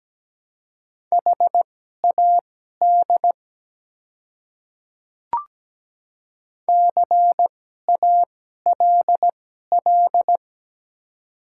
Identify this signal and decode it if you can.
Morse code, heard as a keyed tone, about 17 wpm, spaced as for 12 wpm, 690 Hz. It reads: HAD CALL
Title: 2x Character Spacing - Top 100 Words - 17wpm